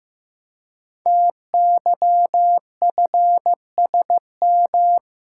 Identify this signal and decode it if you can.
Morse code, heard as a keyed tone, about 15 wpm, 700 Hz. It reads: TYFSM
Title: Code Groups - 5 Letters - 15wpm